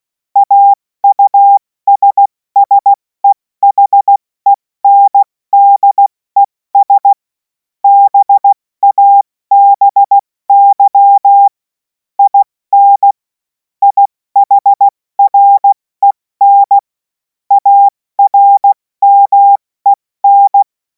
Morse code, heard as a keyed tone, about 16 wpm, 800 Hz. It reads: AUSSEHENDES BABY IN IHREN ARMEN
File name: Morsekurs / Witze / Witz 03. Februar 2026 70 F80